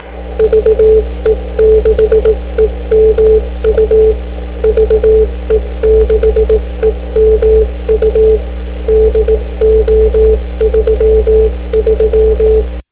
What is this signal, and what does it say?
Morse code, heard as a keyed tone, about 18 wpm, 440 Hz. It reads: VE6EMU VE6EMU DO33